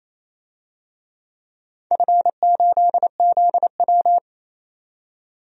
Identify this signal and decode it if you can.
Morse code, heard as a keyed tone, about 28 wpm, 700 Hz. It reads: F8ZW
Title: DXCC-100 Call Signs - 4 Characters - 28wpm